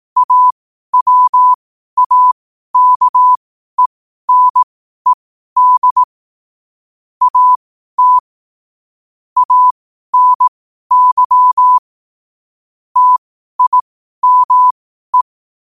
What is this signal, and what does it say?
Morse code, heard as a keyed tone, about 18 wpm, spaced as for 12 wpm, 1000 Hz. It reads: AWAKENED AT ANY TIME
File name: qotdfiles / 12wpm